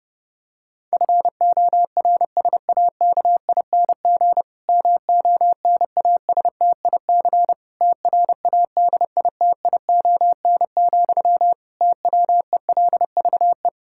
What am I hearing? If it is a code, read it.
Morse code, heard as a keyed tone, about 30 wpm, 700 Hz. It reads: FORSAKING MONASTIC TRADITION, TWELVE
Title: Pangram Challenge v2 - 30wpm